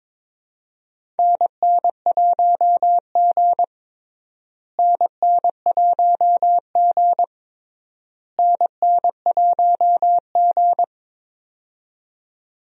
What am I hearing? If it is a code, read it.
Morse code, heard as a keyed tone, about 22 wpm, 700 Hz. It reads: NN1G NN1G NN1G
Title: USA Call Signs - 4 Characters - Repeated 3x - 22wpm